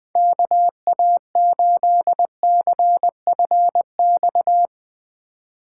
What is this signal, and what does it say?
Morse code, heard as a keyed tone, about 20 wpm, 700 Hz. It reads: KA8CFX
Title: Speed Racing - USA Call Signs 17wpm